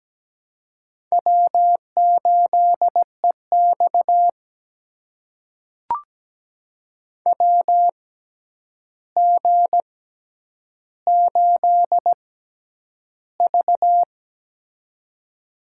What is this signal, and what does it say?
Morse code, heard as a keyed tone, about 17 wpm, spaced as for 9 wpm, 700 Hz. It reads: W8EX WG8V
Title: USA Call Signs - 4 Characters - 6x Character Spacing - 17wpm